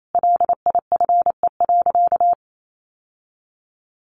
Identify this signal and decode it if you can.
Morse code, heard as a keyed tone, about 28 wpm, 700 Hz. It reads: LIFE.
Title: Sentences from Top 700 Words 28wpm